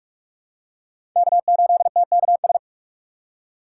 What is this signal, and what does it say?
Morse code, heard as a keyed tone, about 46 wpm, 700 Hz. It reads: K8TKS
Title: USA Call Signs - 5 Characters - 6x Character Spacing - 45wpm